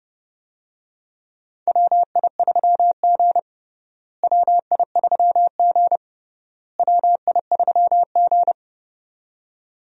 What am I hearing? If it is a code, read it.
Morse code, heard as a keyed tone, about 30 wpm, 700 Hz. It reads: WI3G WI3G WI3G
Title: USA Call Signs - 4 Characters - Repeated 3x - 30wpm